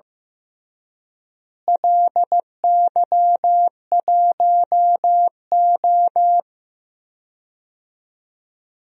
Morse code, heard as a keyed tone, about 15 wpm, 700 Hz. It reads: LY1O
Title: DXCC-100 Call Signs - 4 Characters - Repeated 3x - 15wpm